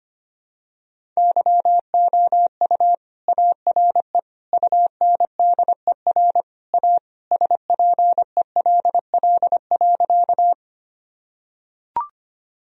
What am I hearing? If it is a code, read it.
Morse code, heard as a keyed tone, about 25 wpm, 700 Hz. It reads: YOU ARE UNDER A SPELL.
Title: Sentences from Top 1000 Words - Review Part 1 - 25wpm